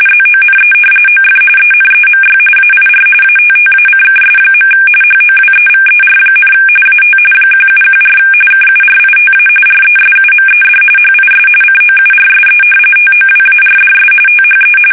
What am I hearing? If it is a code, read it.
RTTY, about 45.45 baud. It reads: ZONE OF ENEMY AIR ATTACKS. HOWEVER, AT 1714 KURITA RESUMED HIS COURSE TOWARDS THE
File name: fsk850 / battleshipnj